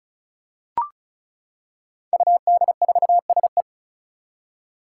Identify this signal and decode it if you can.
Morse code, heard as a keyed tone, about 34 wpm, 700 Hz. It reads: UD4SE
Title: Five Challenging Letter-Number Combinations 35wpm